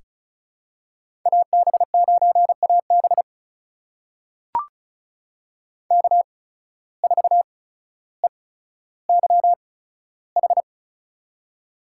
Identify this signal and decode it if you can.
Morse code, heard as a keyed tone, about 36 wpm, spaced as for 15 wpm, 700 Hz. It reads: AB9AB K4EYH
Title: USA Call Signs - 5 Characters - 8x Character Spacing - 35wpm